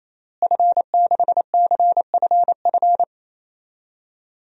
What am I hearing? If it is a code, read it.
Morse code, heard as a keyed tone, about 28 wpm, 700 Hz. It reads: F6CFF
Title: DXCC-100 Call Signs - 5 Characters - Repeated 3x - 28wpm